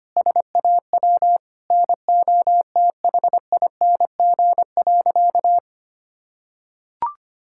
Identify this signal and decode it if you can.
Morse code, heard as a keyed tone, about 25 wpm, 700 Hz. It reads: SAW NOTHING.
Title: Sentences from Top 1000 Words - Review Part 6 - 25wpm